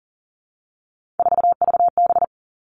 Morse code, heard as a keyed tone, about 42 wpm, 700 Hz. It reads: 4VB
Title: Three Challenging Letter-Number Combinations 40wpm